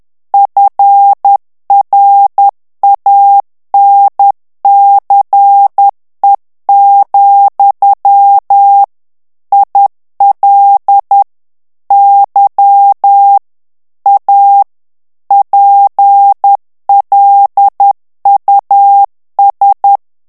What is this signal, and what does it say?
Morse code, heard as a keyed tone, about 10 wpm, 800 Hz. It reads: FRANCE, IL Y A PLUS
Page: Français - texte en clair 1 - vitesse : 10 mots minute